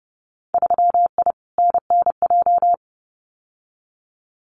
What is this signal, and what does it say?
Morse code, heard as a keyed tone, about 31 wpm, 700 Hz. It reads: 3I NNJ
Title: Winter Field Day Exchanges 30wpm